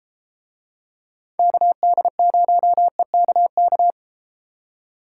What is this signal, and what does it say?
Morse code, heard as a keyed tone, about 33 wpm, 700 Hz. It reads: KD0EKK